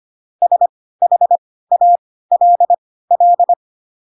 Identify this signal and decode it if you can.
Morse code, heard as a keyed tone, about 25 wpm, spaced as for 16 wpm, 700 Hz. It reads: SHALL
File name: Pinocchio / Pinocchio ch2 10 25 ik1hge